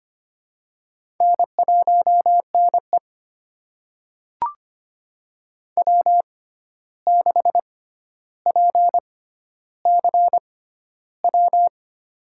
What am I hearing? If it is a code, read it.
Morse code, heard as a keyed tone, about 25 wpm, spaced as for 12 wpm, 700 Hz. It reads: N1NE W6PCW
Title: USA Call Signs - 6x Character Spacing - 25wpm